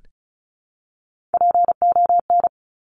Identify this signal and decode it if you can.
Morse code, heard as a keyed tone, about 36 wpm, 700 Hz. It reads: PON
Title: Top Three Letter Combinations in Words 35wpm